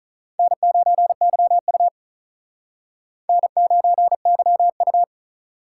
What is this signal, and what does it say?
Morse code, heard as a keyed tone, about 43 wpm, 700 Hz. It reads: N9YU N9YU
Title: Speed Racing - USA Call Signs 35wpm